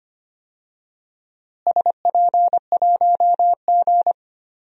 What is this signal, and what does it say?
Morse code, heard as a keyed tone, about 25 wpm, 700 Hz. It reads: SP1G
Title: DXCC-100 Call Signs - 4 Characters - Repeated 2x - 25wpm